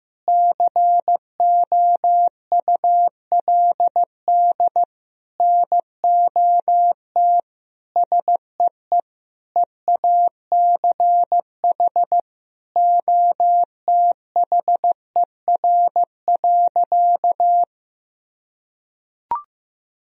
Morse code, heard as a keyed tone, about 16 wpm, 700 Hz. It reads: COULD NOT SEE EACH OTHER.
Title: Sentences from Top 1000 Words - Review Part 1 - 15wpm